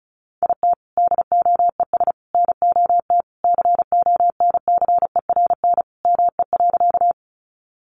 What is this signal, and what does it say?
Morse code, heard as a keyed tone, about 34 wpm, 700 Hz. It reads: IT DOES NOT CONCERN ME.
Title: Sentences from Top 1000 Words 35wpm